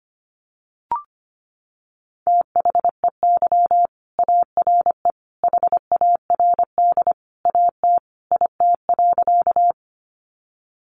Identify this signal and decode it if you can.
Morse code, heard as a keyed tone, about 25 wpm, 700 Hz. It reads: THEY ARE HARD AT IT.
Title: Sentences from Top 300 Words 25wpm